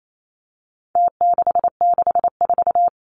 Morse code, heard as a keyed tone, about 28 wpm, 700 Hz. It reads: T664
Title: Numbers - Contest Style - 4-Digit 28wpm